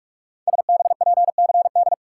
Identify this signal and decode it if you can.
Morse code, heard as a keyed tone, about 44 wpm, 700 Hz. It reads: SBPCD
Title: Code Groups - 5 Letters - 45wpm